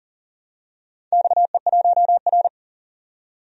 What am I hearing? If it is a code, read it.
Morse code, heard as a keyed tone, about 38 wpm, 700 Hz. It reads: XE1R
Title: DXCC-100 Call Signs - 4 Characters - 40wpm